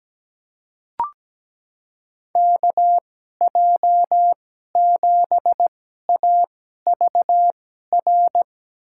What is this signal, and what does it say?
Morse code, heard as a keyed tone, about 17 wpm, spaced as for 12 wpm, 700 Hz. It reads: KJ7AVR
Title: USA Call Signs - 2x Character Spacing - 17wpm